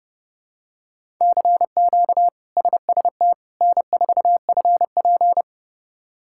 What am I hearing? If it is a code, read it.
Morse code, heard as a keyed tone, about 32 wpm, 700 Hz. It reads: CQ SST N4FP
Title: K1USN SST QSO Practice - 30wpm